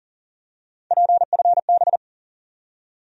Morse code, heard as a keyed tone, about 40 wpm, 700 Hz. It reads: PFB
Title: USA Call Sign Postfixes - 3 Letters 40wpm